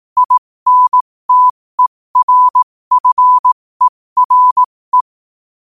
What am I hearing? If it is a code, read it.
Morse code, heard as a keyed tone, about 18 wpm, spaced as for 15 wpm, 1000 Hz. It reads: INTERFERE
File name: qotdfiles / 15wpm